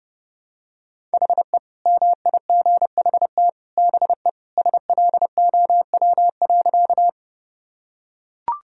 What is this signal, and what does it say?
Morse code, heard as a keyed tone, about 30 wpm, 700 Hz. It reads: HE MIGHT BE SLOW.
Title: Sentences from Top 1000 Words - Review Part 2 - 30wpm